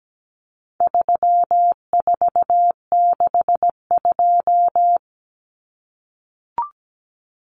Morse code, heard as a keyed tone, about 17 wpm, 700 Hz. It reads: 3462
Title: Common Cut Numbers - 4-Digit 17wpm